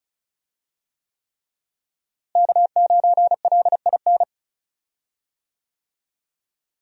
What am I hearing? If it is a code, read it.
Morse code, heard as a keyed tone, about 36 wpm, 700 Hz. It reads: K9LIN
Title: USA Call Signs - 5 Characters - Repeated 3x - 35wpm